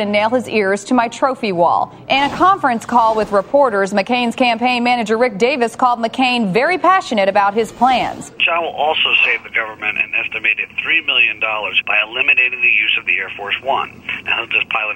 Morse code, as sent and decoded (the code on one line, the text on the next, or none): none